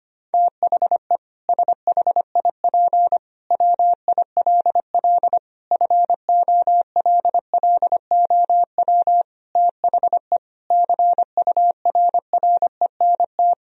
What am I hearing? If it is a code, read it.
Morse code, heard as a keyed tone, about 25 wpm, 700 Hz. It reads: THE SHIP WILL FOLLOW THE CURRENT
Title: Sentences from Top 1000 Words - Review Part 4 - 25wpm